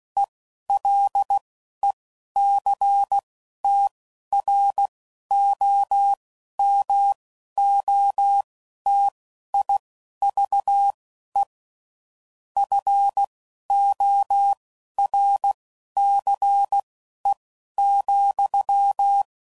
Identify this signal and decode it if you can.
Morse code, heard as a keyed tone, about 16 wpm, spaced as for 11 wpm, 800 Hz. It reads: ELECTROMOTIVE FORCE,
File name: CW Code Groups @10WPM / Technical Subjects / XL